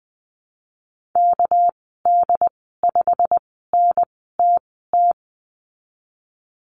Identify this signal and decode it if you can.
Morse code, heard as a keyed tone, about 20 wpm, spaced as for 15 wpm, 700 Hz. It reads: KD5NTT
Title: 200 Lesson 32g - Letter k - Call Signs 20wpm